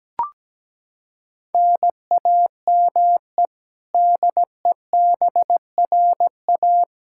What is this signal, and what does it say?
Morse code, heard as a keyed tone, about 17 wpm, 700 Hz. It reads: NAME DEBRA
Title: Practice QSO - Exchange 1 - Beginning - 17wpm